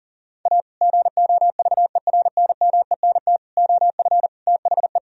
Morse code, heard as a keyed tone, about 40 wpm, 700 Hz. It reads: A GOVERNMENT OF THE
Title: Sentences from Top 1000 Words - Review Part 3 - 40wpm